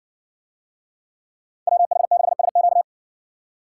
Morse code, heard as a keyed tone, about 60 wpm, 700 Hz. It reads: WH6SQ